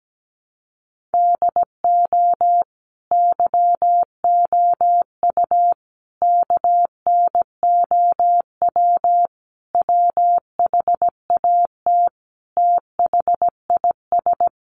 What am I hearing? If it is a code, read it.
Morse code, heard as a keyed tone, about 17 wpm, 700 Hz. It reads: DO YOU KNOW WHAT THIS
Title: Sentences from Top 1500 Words 17wpm